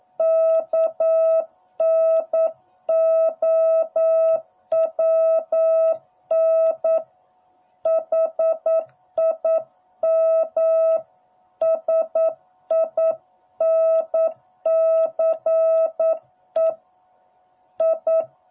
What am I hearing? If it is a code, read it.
Morse code, heard as a keyed tone, about 9 wpm, 660 Hz. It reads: KNOWN HIM SINCE I